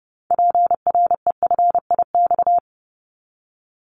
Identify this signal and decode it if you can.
Morse code, heard as a keyed tone, about 30 wpm, 700 Hz. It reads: PREFIX
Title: Callsign Prefixes for Top 100 DXCC - 30wpm